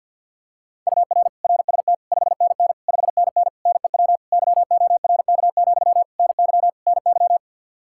Morse code, heard as a keyed tone, about 48 wpm, 700 Hz. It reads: UR RST 5NN 5NN NEW YORK, NY NY
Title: Practice QSO - Exchange 1 - Beginning - 50wpm